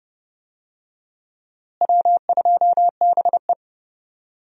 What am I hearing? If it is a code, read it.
Morse code, heard as a keyed tone, about 30 wpm, 700 Hz. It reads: W2BE
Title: USA Call Signs - 4 Characters - Repeated 2x - 30wpm